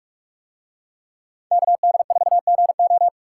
Morse code, heard as a keyed tone, about 42 wpm, 700 Hz. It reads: KD4GO